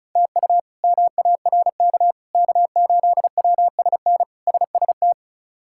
Morse code, heard as a keyed tone, about 34 wpm, 700 Hz. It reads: TU MARK K8WSN SST
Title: K1USN SST QSO Practice - 35wpm